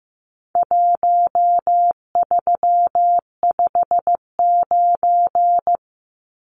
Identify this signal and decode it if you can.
Morse code, heard as a keyed tone, about 15 wpm, 700 Hz. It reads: 1359
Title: Numbers - Contest Style - 4-Digit 15wpm